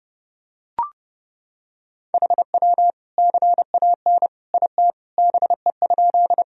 Morse code, heard as a keyed tone, about 30 wpm, 700 Hz. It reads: HW CAN IT BE?
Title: Sentences from Top 100 Words and Abbreviations 30wpm